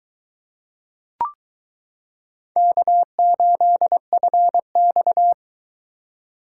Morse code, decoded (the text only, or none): K8FX